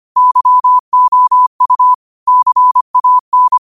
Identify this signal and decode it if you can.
Morse code, heard as a keyed tone, about 25 wpm, 1000 Hz. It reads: YOU CAN
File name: qotdfiles / 25wpm